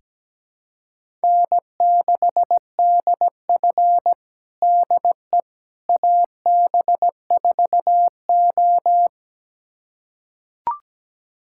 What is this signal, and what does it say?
Morse code, heard as a keyed tone, about 17 wpm, 700 Hz. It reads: N6DF DE AB4O
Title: USA Callsign DE Callsign - 4 Characters - 17wpm